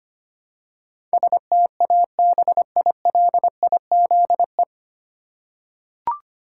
Morse code, heard as a keyed tone, about 25 wpm, 700 Hz. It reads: STABILIZE
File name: Nine Letter Words - 25wpm